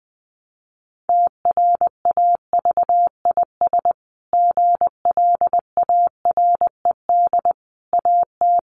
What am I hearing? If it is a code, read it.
Morse code, heard as a keyed tone, about 19 wpm, 700 Hz. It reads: TRAVIS GLARED AT
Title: A Sound Of Thunder 20wpm